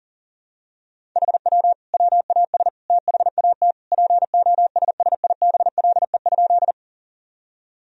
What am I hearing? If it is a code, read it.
Morse code, heard as a keyed tone, about 40 wpm, 700 Hz. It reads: HW WAS THAT POSSIBLE?
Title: Sentences from Top 500 Words and Abbreviations 40wpm